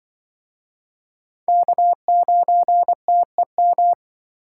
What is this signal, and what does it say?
Morse code, heard as a keyed tone, about 26 wpm, 700 Hz. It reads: K9TEM